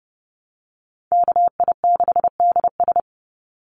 Morse code, heard as a keyed tone, about 32 wpm, 700 Hz. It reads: KI6DS
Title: NA SOTA Chaser Callsigns 30wpm